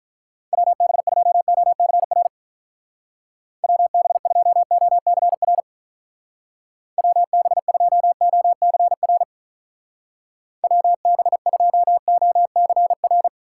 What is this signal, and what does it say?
Morse code, heard as a keyed tone, about 43 wpm, 700 Hz. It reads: WB2OCR WB2OCR WB2OCR WB2OCR
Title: Speed Racing - USA 2x3 Call Signs 35wpm